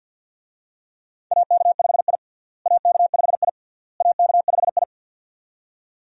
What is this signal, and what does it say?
Morse code, heard as a keyed tone, about 52 wpm, 700 Hz. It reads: AK5I AK5I AK5I
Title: USA Call Signs - 4 Characters - Repeated 3x - 50wpm